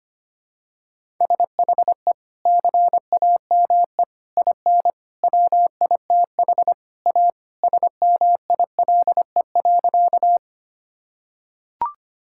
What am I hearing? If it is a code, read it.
Morse code, heard as a keyed tone, about 25 wpm, 700 Hz. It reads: SHE CAME IN WITH A SMILE.
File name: practice / 25wpm / Sentences with 2-Word Phrases - 25wpm